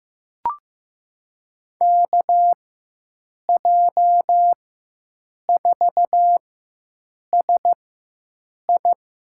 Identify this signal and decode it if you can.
Morse code, heard as a keyed tone, about 15 wpm, spaced as for 7 wpm, 700 Hz. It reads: KJ4SI
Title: USA Call Signs - 6 Characters - 4x Character Spacing - 15wpm